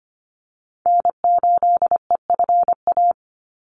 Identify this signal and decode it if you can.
Morse code, heard as a keyed tone, about 25 wpm, 700 Hz. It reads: N8EFA
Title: USA Call Signs 25wpm